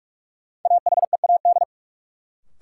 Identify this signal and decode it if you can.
Morse code, heard as a keyed tone, about 43 wpm, 700 Hz. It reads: AHEAD
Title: Sentences from Top 1000 Words - Review Part 6 - New Word Drill - 45wpm